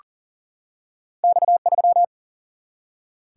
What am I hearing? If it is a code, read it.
Morse code, heard as a keyed tone, about 41 wpm, 700 Hz. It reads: X 3
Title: Single-Letter Single-Number 40wpm